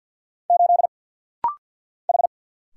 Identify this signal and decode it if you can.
Morse code, heard as a keyed tone, about 50 wpm, 700 Hz. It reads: 8 H
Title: Single Letter-Number Alt - Rapid-Fire 50wpm